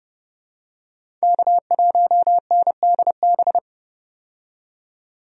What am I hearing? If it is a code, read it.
Morse code, heard as a keyed tone, about 32 wpm, 700 Hz. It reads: K1NDB
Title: USA Call Signs - 5 Characters - 30wpm